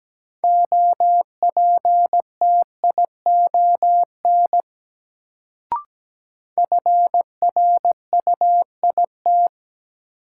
Morse code, heard as a keyed tone, about 17 wpm, 700 Hz. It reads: OPTION FRUIT
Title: Sentences from Top 1300 Words - New Word Drill - 17wpm